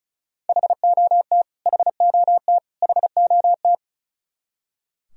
Morse code, heard as a keyed tone, about 34 wpm, 700 Hz. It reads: HOT HOT HOT
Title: Top 100 Words Repeated 35wpm